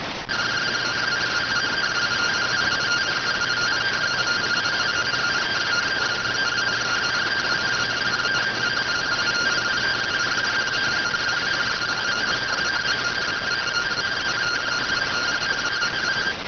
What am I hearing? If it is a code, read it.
Two-tone teleprinter signal, about 170 Hz shift.